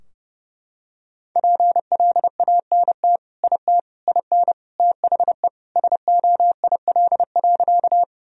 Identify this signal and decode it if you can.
Morse code, heard as a keyed tone, about 30 wpm, 700 Hz. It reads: PLANT IT IN THE SOIL.
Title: Sentences from Top 1000 Words - Review Part 4 - 30wpm